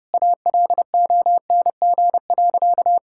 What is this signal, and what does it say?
Morse code, heard as a keyed tone, about 32 wpm, 700 Hz. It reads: ALONG.
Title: Sentences from Top 1600 Words 30wpm